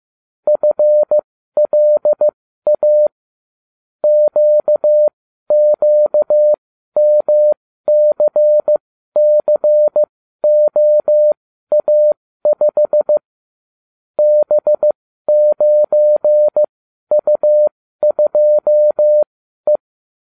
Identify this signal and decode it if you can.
Morse code, heard as a keyed tone, about 15 wpm, spaced as for 12 wpm, 600 Hz. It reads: FLA QQMCCOA5 B9U2E
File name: cw / ubacw35 3